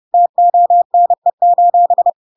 Morse code, heard as a keyed tone, about 30 wpm, 700 Hz. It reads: TONE:
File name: Pinocchio / Pinocchio ch1 30 30 ik1hge